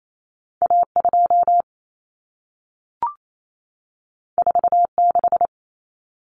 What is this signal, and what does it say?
Morse code, heard as a keyed tone, about 28 wpm, 700 Hz. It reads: A2 46
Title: Common Cut Numbers - 2-Digit 28wpm